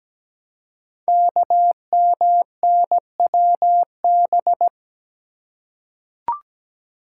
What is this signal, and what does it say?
Morse code, heard as a keyed tone, about 17 wpm, 700 Hz. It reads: KMNWB
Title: Code Groups - 5 Letters - 17wpm